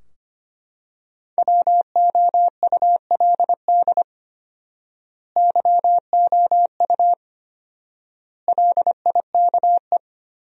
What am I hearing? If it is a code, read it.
Morse code, heard as a keyed tone, about 25 wpm, 700 Hz. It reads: WOULD YOU LIKE
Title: 4x Word-Spacing - Sentences from Top 100 Words - 25wpm